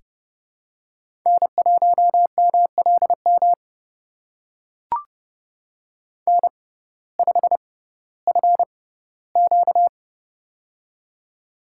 Morse code, heard as a keyed tone, about 30 wpm, spaced as for 17 wpm, 700 Hz. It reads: N1MLM N5FQ
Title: USA Call Signs - 6x Character Spacing - 30wpm